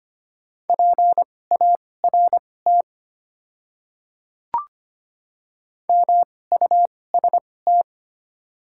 Morse code, heard as a keyed tone, about 25 wpm, spaced as for 18 wpm, 690 Hz. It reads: PART MUST
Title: 2x Character Spacing - Top 300 Words - 25wpm